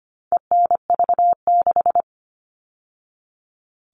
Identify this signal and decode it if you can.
Morse code, heard as a keyed tone, about 25 wpm, 700 Hz. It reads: ENV6
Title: All Cut Numbers - 4-Digit 25wpm